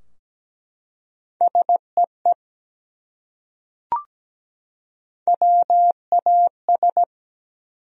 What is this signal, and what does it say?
Morse code, heard as a keyed tone, about 17 wpm, 700 Hz. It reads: SEE WAS
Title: Top 100 Words 17wpm